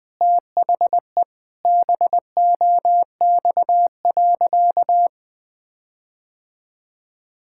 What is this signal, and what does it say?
Morse code, heard as a keyed tone, about 20 wpm, 700 Hz. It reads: THE BOX.
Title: Sentences with 4-Word Phrases - Spoken Beginning - 20wpm